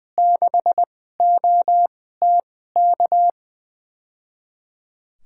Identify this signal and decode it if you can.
Morse code, heard as a keyed tone, about 20 wpm, spaced as for 14 wpm, 700 Hz. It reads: 6OTK
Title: USA Call Signs - 2x Character Spacing - 20wpm